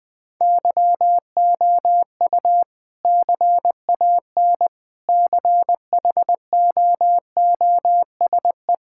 Morse code, heard as a keyed tone, about 20 wpm, 700 Hz. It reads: YOU CAN CHOOSE
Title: Sentences with 2-Word Phrases - Spoken Beginning - 20wpm